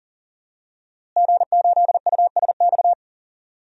Morse code, heard as a keyed tone, about 40 wpm, 700 Hz. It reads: G8USX
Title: DXCC-100 Call Signs - 5 Characters - Repeated 3x - 40wpm